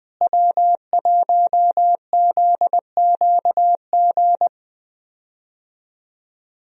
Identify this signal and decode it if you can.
Morse code, heard as a keyed tone, about 20 wpm, 700 Hz. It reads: W1ZQG
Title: USA Call Signs - 5 Characters - Repeated 3x - 20wpm